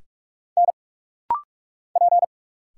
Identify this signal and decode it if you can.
Morse code, heard as a keyed tone, about 45 wpm, 700 Hz. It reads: N P
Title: Single Letters - Rapid-Fire 45wpm